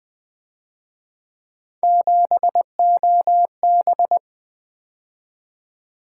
Morse code, heard as a keyed tone, about 24 wpm, 700 Hz. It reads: 7OB
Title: Speed-Racing - Three Letter-Number Combinations - 20wpm